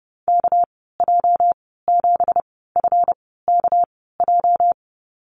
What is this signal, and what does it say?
Morse code, heard as a keyed tone, about 30 wpm, spaced as for 17 wpm, 700 Hz. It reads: KJ7FKJ
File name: Lesson 37g - Letter x - Call Signs 30wpm